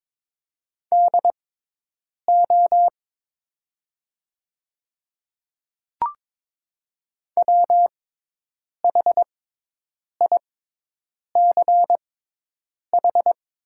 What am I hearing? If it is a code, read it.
Morse code, heard as a keyed tone, about 22 wpm, spaced as for 8 wpm, 690 Hz. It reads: DO WHICH
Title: 6x Character Spacing - Top 100 Words - 22wpm